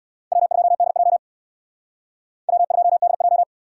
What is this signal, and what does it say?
Morse code, heard as a keyed tone, about 57 wpm, 700 Hz. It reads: W2DJ W2DJ
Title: Speed Racing - USA 1x2 Call Signs 50wpm